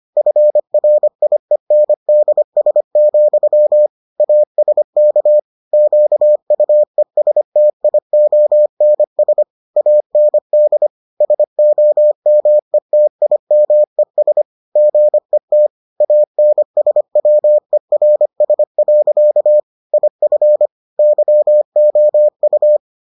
Morse code, heard as a keyed tone, about 25 wpm, 600 Hz. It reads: FRIENDS, ASK QUESTIONS AND SOMETIMES GET ANSWERS. IF YOU